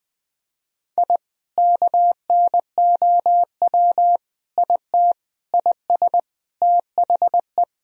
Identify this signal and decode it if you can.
Morse code, heard as a keyed tone, about 21 wpm, 700 Hz. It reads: I KNOW IT IS THE
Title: Sentences from Top 1000 Words - Review Part 3 - 20wpm